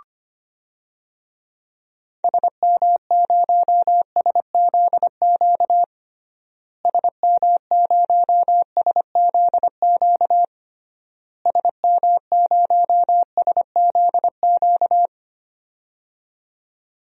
Morse code, heard as a keyed tone, about 25 wpm, 700 Hz. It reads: SM0SZQ SM0SZQ SM0SZQ
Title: DXCC-100 Call Signs - 6 Characters - Repeated 3x - 25wpm